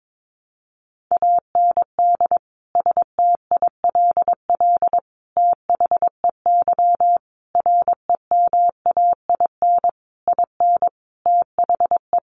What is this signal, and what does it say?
Morse code, heard as a keyed tone, about 22 wpm, 700 Hz. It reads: AND STILL THEY REMAIN IN THE
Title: Sentences from Top 1200 Words 22wpm